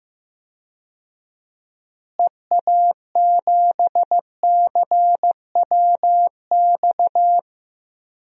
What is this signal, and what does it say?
Morse code, heard as a keyed tone, about 15 wpm, 700 Hz. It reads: EA7CWX
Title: DXCC-100 Call Signs - 6 Characters - Repeated 2x - 15wpm